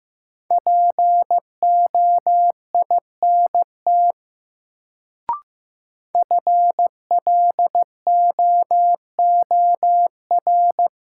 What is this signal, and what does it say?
Morse code, heard as a keyed tone, about 15 wpm, 700 Hz. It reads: POINT FLOOR
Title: Sets of 3 Words - New Word Drill - 15wpm